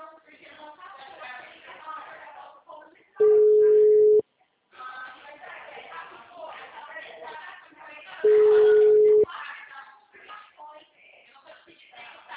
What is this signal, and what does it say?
Ringback tone (Europe, 425 Hz), with speech.
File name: out- -2011- - - .64960